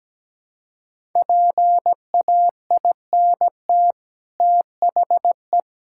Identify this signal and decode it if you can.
Morse code, heard as a keyed tone, about 17 wpm, 700 Hz. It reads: PAINT THE
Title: Sentences from Top 1000 Words - Review Part 6 - 17wpm